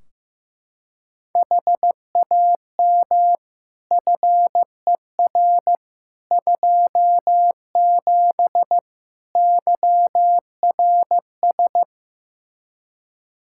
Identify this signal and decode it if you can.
Morse code, heard as a keyed tone, about 15 wpm, 700 Hz. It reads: HAM FER 27 YRS
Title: Practice QSO - Exchange 3 - More Info - 15wpm